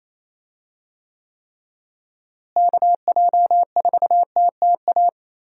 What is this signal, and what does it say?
Morse code, heard as a keyed tone, about 27 wpm, 700 Hz. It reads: KJ4TTA
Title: USA Call Signs - 6 Characters - Repeated 3x - 28wpm